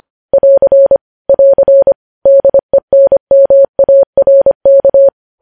Morse code, heard as a keyed tone, about 25 wpm, 550 Hz. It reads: <AR> <AR> DENMARK
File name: podcast / News headlines-1x-25x25